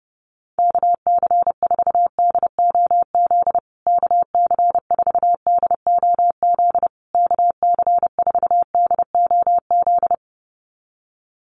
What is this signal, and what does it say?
Morse code, heard as a keyed tone, about 30 wpm, 700 Hz. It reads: KC4DOZ KC4DOZ KC4DOZ
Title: USA Call Signs Repeated 3x 30wpm